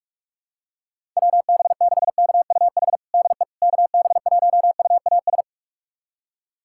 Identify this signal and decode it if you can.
Morse code, heard as a keyed tone, about 46 wpm, 700 Hz. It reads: WB6KUH DE KB1UAS
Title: USA Callsign DE Callsign - 6 Characters - 45wpm